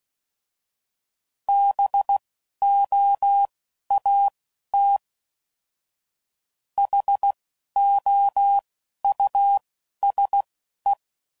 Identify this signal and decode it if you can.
Morse code, heard as a keyed tone, about 16 wpm, spaced as for 11 wpm, 800 Hz. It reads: BOAT HOUSE
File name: Suffixes / General / Houses